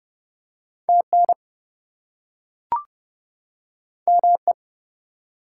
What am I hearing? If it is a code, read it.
Morse code, heard as a keyed tone, about 30 wpm, 700 Hz. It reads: TN ME
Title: US States and Canadian Provinces 30wpm